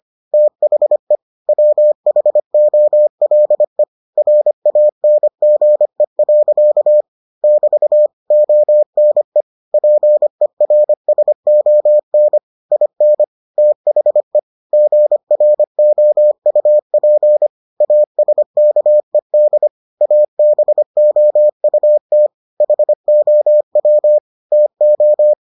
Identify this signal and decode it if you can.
Morse code, heard as a keyed tone, about 25 wpm, 600 Hz. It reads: THE WHOLE RANGE. <BT> ONE PERSON IN THE GROUP ASKED ABOUT HOW TO